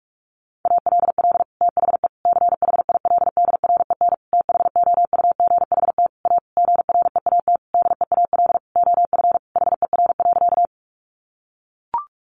Morse code, heard as a keyed tone, about 44 wpm, 700 Hz. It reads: ALL THE CHILDREN THOUGHT A GREAT DEAL OF HER.
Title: Sentences from Top 800 Words 45wpm